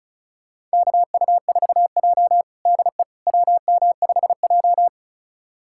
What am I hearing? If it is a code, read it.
Morse code, heard as a keyed tone, about 36 wpm, 700 Hz. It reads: KU4J DE WM5J
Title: USA Callsign DE Callsign - 4 Characters - 35wpm